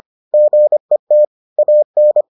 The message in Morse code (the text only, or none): GET AN